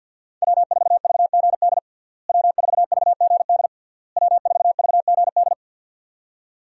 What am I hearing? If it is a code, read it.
Morse code, heard as a keyed tone, about 51 wpm, 700 Hz. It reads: W4VGD W4VGD W4VGD
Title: USA Call Signs - 5 Characters - Repeated 3x - 50wpm